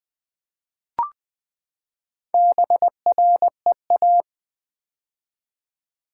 Morse code, heard as a keyed tone, about 20 wpm, 700 Hz. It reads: BREA
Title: Build A Word - 4 Letters At A Time - 12-Letter Words - 20wpm